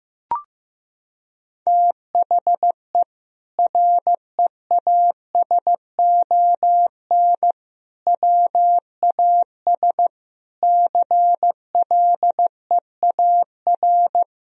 Morse code, heard as a keyed tone, about 15 wpm, 700 Hz. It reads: THE REASON WAS CLEAR
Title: Sentences from Top 1000 Words - Review Part 2 - 15wpm